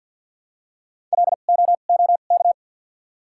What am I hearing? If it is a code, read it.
Morse code, heard as a keyed tone, about 54 wpm, spaced as for 37 wpm, 700 Hz. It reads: LOOK
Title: 2x Character Spacing - Top 200 Words - 50wpm